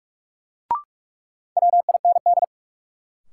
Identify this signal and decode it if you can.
Morse code, heard as a keyed tone, about 41 wpm, 700 Hz. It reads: WIND
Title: Sentences from Top 400 Words - New Word Drill - 45wpm